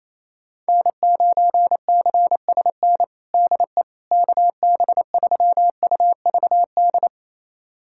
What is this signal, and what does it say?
Morse code, heard as a keyed tone, about 27 wpm, 700 Hz. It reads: N9CSN DE KB3UVD
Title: USA Callsign DE Callsign 28wpm